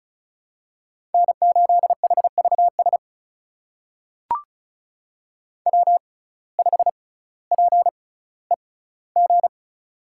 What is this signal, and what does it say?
Morse code, heard as a keyed tone, about 36 wpm, spaced as for 18 wpm, 700 Hz. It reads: N8HVS W5PEG
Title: USA Call Signs - 5 Characters - 6x Character Spacing - 35wpm